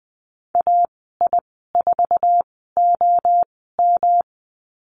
Morse code, heard as a keyed tone, about 20 wpm, spaced as for 15 wpm, 700 Hz. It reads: AI4OM
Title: 207 Lesson 33g - Letter j - Call Signs 20wpm